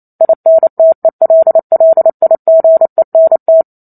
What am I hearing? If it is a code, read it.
Morse code, heard as a keyed tone, about 28 wpm, 650 Hz. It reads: INTELLIGENT